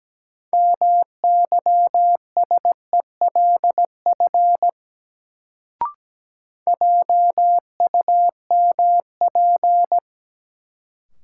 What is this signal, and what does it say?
Morse code, heard as a keyed tone, about 17 wpm, 700 Hz. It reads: MYSELF JUMP
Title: Sentences from Top 1300 Words - New Word Drill Alt - 17wpm